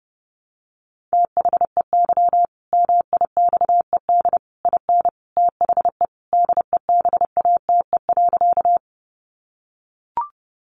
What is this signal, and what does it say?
Morse code, heard as a keyed tone, about 30 wpm, 700 Hz. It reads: THEY MIXED IN THE DEBATE.
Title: Sentences from Top 1500 Words 30wpm